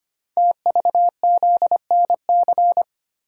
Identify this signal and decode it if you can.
Morse code, heard as a keyed tone, about 25 wpm, 700 Hz. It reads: TVZNC
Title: Code Groups - 5 Letter:Numbers - 25wpm